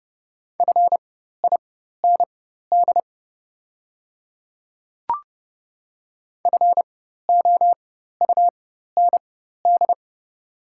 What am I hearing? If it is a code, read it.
Morse code, heard as a keyed tone, about 31 wpm, spaced as for 14 wpm, 700 Hz. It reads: FIND FOUND
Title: 4x Character Spacing - Top 300 Words - 30wpm